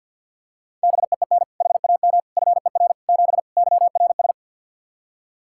Morse code, heard as a keyed tone, about 52 wpm, 700 Hz. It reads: BEEN HAM FER 7 YRS
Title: K1USN SST QSO Practice 50wpm